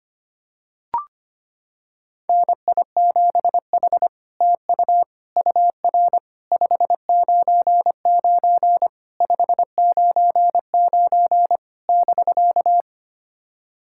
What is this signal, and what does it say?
Morse code, heard as a keyed tone, about 25 wpm, 700 Hz. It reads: NI7H TU UR 599 599 <BK>
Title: SOTA QSO Practice 25wpm